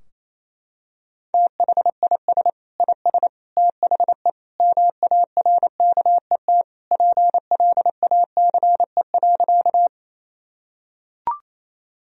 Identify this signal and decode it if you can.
Morse code, heard as a keyed tone, about 28 wpm, 700 Hz. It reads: THIS IS THE MARKET PLACE.
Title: Sentences from Top 1000 Words - Review Part 3 - 28wpm